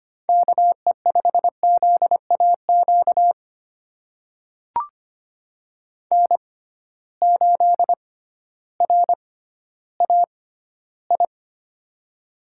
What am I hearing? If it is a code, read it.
Morse code, heard as a keyed tone, about 25 wpm, spaced as for 14 wpm, 700 Hz. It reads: KE5ZAQ N8RAI
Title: USA Call Signs - 6x Character Spacing - 25wpm